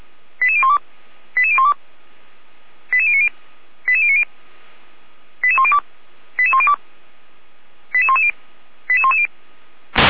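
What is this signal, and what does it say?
Five-tone selective call (ZVEI), heard as quick runs of five tones, tones of about 80 ms. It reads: 88012 88012 88090 88090 88102 88102 88110 88110
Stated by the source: ZVEI five-tone call received on SDR by the Orlamünde fire brigade (feuerwehr-orlamuende).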